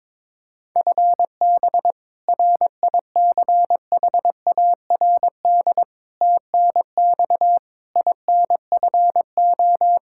Morse code, heard as a keyed tone, about 22 wpm, 700 Hz. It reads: FB RICHARD TNX INFO
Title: Practice QSO - Exchange 3 - More Info - 22wpm